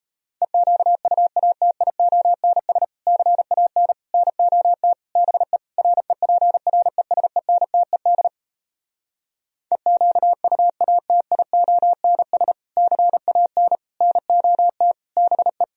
Morse code, heard as a keyed tone, about 38 wpm, 700 Hz. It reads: EQUATIONS CAN NOT BE REPRESENTED EQUATIONS CAN NOT BE